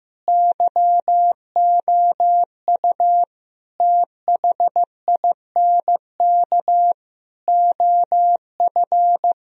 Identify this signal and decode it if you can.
Morse code, heard as a keyed tone, about 15 wpm, 700 Hz. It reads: YOU THINK OF
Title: Sentences from Top 1000 Words - Review Part 4 - 15wpm